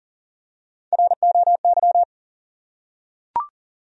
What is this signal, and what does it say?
Morse code, heard as a keyed tone, about 42 wpm, 700 Hz. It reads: ROY
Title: USA Common Names - Max 3 Letters - 40wpm